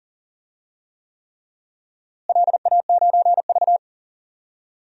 Morse code, heard as a keyed tone, about 40 wpm, 700 Hz. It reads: LA9V
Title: DXCC-100 Call Signs - 4 Characters - Repeated 3x - 40wpm